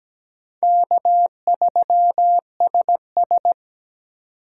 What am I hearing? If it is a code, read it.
Morse code, heard as a keyed tone, about 20 wpm, 700 Hz. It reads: K3SS
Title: Speed Racing - USA 1x2 Call Signs 17wpm